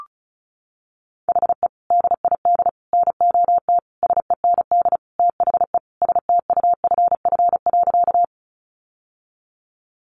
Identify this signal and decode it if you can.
Morse code, heard as a keyed tone, about 34 wpm, 700 Hz. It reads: HE DID NOT SEND THE STUFF.
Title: Sentences from Top 1500 Words 35wpm